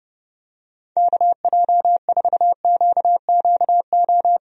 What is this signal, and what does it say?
Morse code, heard as a keyed tone, about 32 wpm, 700 Hz. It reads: KJ4QQO
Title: USA Call Signs - 6 Characters - 30wpm